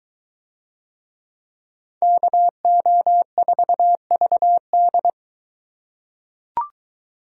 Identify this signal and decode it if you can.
Morse code, heard as a keyed tone, about 23 wpm, 700 Hz. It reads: KO4VD